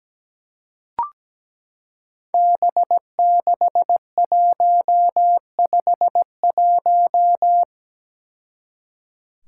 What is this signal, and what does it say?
Morse code, heard as a keyed tone, about 17 wpm, 700 Hz. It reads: B6151
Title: Five Challenging Letter-Number Combinations 17wpm